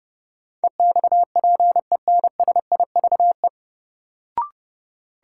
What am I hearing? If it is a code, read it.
Morse code, heard as a keyed tone, about 30 wpm, 700 Hz. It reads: EXPENSIVE
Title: Sentences from Top 1000 Words - Review Part 6 - New Word Drill - 30wpm